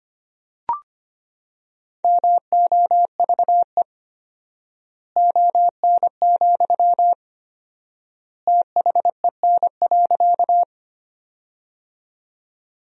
Morse code, heard as a keyed tone, about 25 wpm, 700 Hz. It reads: MOVE ON, THEN.
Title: 4x Word-Spacing - Sentences from Top 200 Words - 25wpm